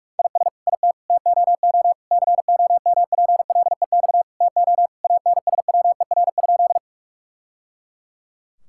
Morse code, heard as a keyed tone, about 41 wpm, 700 Hz. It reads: IS IT TOO COMPLEX TO ANSWER?
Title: Sentences from Top 1000 Words - Review Part 6 - 45wpm